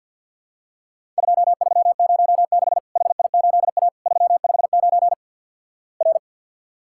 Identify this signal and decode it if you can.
Morse code, heard as a keyed tone, about 50 wpm, 700 Hz. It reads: 2306 HI8A 359 R
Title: Worked All Europe DX Contest - QTC Exchange with Repeats - 50wpm